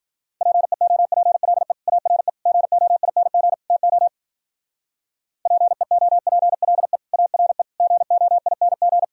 Morse code, heard as a keyed tone, about 56 wpm, 700 Hz. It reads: PEOPLE ARE GOING TO PEOPLE ARE GOING